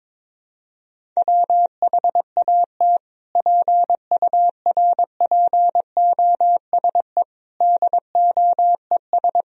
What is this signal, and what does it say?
Morse code, heard as a keyed tone, about 22 wpm, 700 Hz. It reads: WHAT PURPOSE DOES
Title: Sentences from Top 1000 Words - Review Part 2 - 22wpm